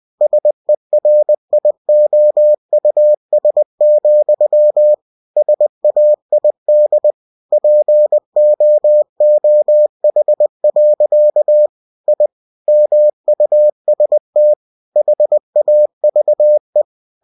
Morse code, heard as a keyed tone, about 20 wpm, 600 Hz. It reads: SERIOUS, SAID POOH. I MUST HAVE
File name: cw-books / winnie-the-pooh / winnie-20wpm-page-0094